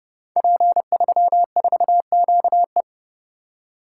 Morse code, heard as a keyed tone, about 28 wpm, 700 Hz. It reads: P34QE
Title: Code Groups - 5 Letter:Numbers - 30wpm